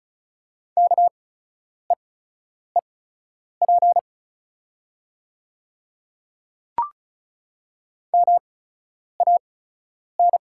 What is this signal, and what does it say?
Morse code, heard as a keyed tone, about 36 wpm, spaced as for 10 wpm, 700 Hz. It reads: KEEP MAN
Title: 8x Character Spacing - Top 500 Words - 35wpm